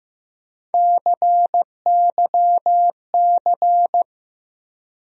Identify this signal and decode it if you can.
Morse code, heard as a keyed tone, about 15 wpm, 700 Hz. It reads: CYC
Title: USA Call Sign Postfixes - 3 Letters 15wpm